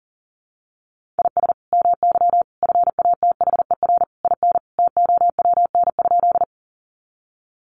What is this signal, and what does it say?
Morse code, heard as a keyed tone, about 40 wpm, 700 Hz. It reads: IS MY FATHER IN TOWN?
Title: Sentences from Top 500 Words 40wpm